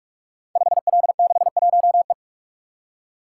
Morse code, heard as a keyed tone, about 44 wpm, 700 Hz. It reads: 5L61E
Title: Code Groups - 5 Letter:Numbers - 45wpm